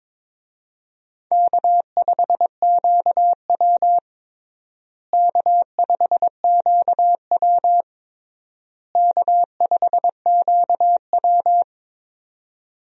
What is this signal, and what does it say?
Morse code, heard as a keyed tone, about 22 wpm, 700 Hz. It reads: K5QW K5QW K5QW
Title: USA Call Signs - 4 Characters - Repeated 3x - 22wpm